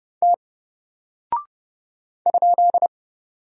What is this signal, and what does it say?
Morse code, heard as a keyed tone, about 30 wpm, 700 Hz. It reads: T ?
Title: Single Character (Letter:Number:Punctuation:Prosign) Alt 30wpm